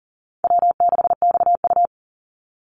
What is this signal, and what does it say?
Morse code, heard as a keyed tone, about 40 wpm, 700 Hz. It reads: W6XU
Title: Call Signs Repeated 3x - 40wpm